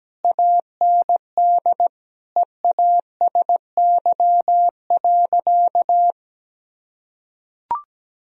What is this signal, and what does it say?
Morse code, heard as a keyed tone, about 17 wpm, 700 Hz. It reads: AND EASY.
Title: Sentences from Top 1000 Words - Review Part 3 - 17wpm